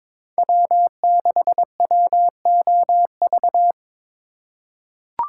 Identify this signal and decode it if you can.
Morse code, heard as a keyed tone, about 22 wpm, 700 Hz. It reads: W6WOV
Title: USA Call Signs - 5 Characters - 2x Character Spacing - 22wpm